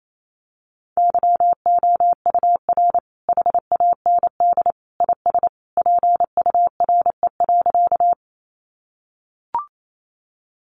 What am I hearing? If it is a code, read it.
Morse code, heard as a keyed tone, about 28 wpm, 700 Hz. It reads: YOUR HAND IS PURE.
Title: Sentences from Top 1500 Words 28wpm